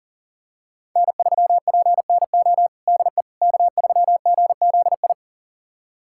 Morse code, heard as a keyed tone, about 40 wpm, 700 Hz. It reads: N3PNO DE K3GZI
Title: USA Callsign DE Callsign - 5 Characters - 40wpm